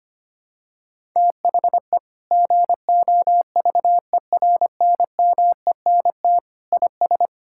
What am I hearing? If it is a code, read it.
Morse code, heard as a keyed tone, about 25 wpm, 700 Hz. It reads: THE GOVERNMENT IS
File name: practice / 25wpm / Sentences with 9-Word Phrases - 25wpm